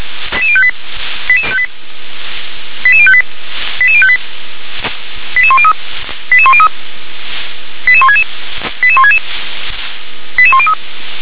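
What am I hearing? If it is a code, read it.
Five-tone selective call (ZVEI), heard as quick runs of five tones, tones of about 70 ms. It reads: 88058 88058 88058 88103 88103 88166 88166 88103
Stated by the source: ZVEI five-tone selective call 88006 received by SDR.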